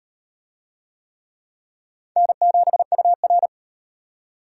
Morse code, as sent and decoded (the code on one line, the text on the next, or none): -. --... ..- .-.
N7UR